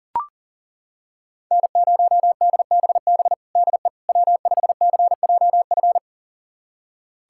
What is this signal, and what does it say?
Morse code, heard as a keyed tone, about 37 wpm, 700 Hz. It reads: N0DBB DE W5CJF
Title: USA Callsign DE Callsign - 5 Characters - 40wpm